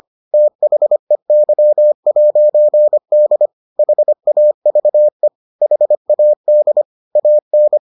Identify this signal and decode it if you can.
Morse code, heard as a keyed tone, about 25 wpm, 600 Hz. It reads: THEY'D HAVE HAD AN